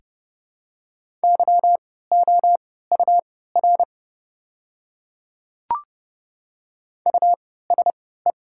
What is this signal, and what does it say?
Morse code, heard as a keyed tone, about 30 wpm, spaced as for 17 wpm, 690 Hz. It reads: YOUR USE
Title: 3x Character Spacing - Top 500 Words - 30wpm